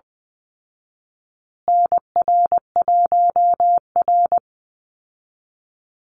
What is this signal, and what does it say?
Morse code, heard as a keyed tone, about 20 wpm, 700 Hz. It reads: NR1R
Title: Call Signs Repeated 2x - 20wpm